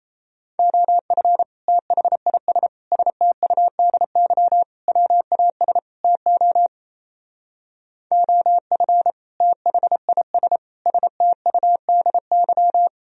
Morse code, heard as a keyed tone, about 37 wpm, 700 Hz. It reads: OF THIS STUDY WAS TO OF THIS STUDY